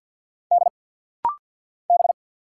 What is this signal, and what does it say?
Morse code, heard as a keyed tone, about 50 wpm, 690 Hz. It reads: D B
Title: Single Letters - Rapid-Fire 50wpm